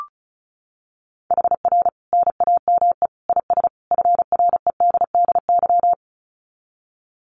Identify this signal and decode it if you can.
Morse code, heard as a keyed tone, about 34 wpm, 700 Hz. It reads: HR NAME IS FREDDY
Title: Common QSO Phrases 35wpm